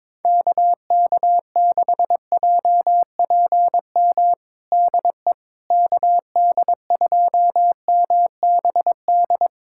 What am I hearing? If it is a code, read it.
Morse code, heard as a keyed tone, about 22 wpm, 700 Hz. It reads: KK6JPM DE KD2MBD
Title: USA Callsign DE Callsign - 6 Characters - 22wpm